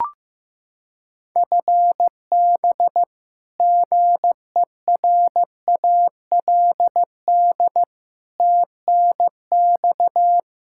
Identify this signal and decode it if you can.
Morse code, heard as a keyed tone, about 15 wpm, 700 Hz. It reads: FB GERALD TNX
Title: Practice QSO - Exchange 3 - More Info - 15wpm